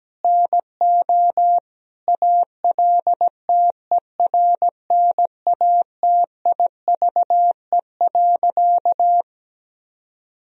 Morse code, heard as a keyed tone, about 17 wpm, 700 Hz. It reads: NO ALTERNATIVE.
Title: Sentences from Top 1000 Words - Review Part 4 - 17wpm